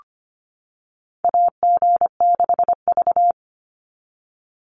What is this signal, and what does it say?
Morse code, heard as a keyed tone, about 25 wpm, 700 Hz. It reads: AG6V
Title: NA SOTA Chaser Callsigns 25wpm